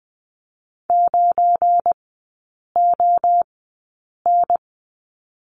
Morse code, heard as a keyed tone, about 20 wpm, spaced as for 8 wpm, 700 Hz. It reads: 9ON
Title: 178 Lesson 29e - Slash - Three Characters 20wpm